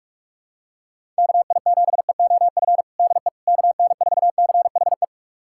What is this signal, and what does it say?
Morse code, heard as a keyed tone, about 41 wpm, 700 Hz. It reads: KI7EOF DE KN4CHE
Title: USA Callsign DE Callsign - 6 Characters - 45wpm